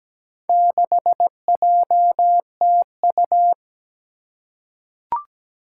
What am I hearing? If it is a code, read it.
Morse code, heard as a keyed tone, about 17 wpm, 700 Hz. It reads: 6JTU
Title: Four Challenging Letter-Number Combinations 17wpm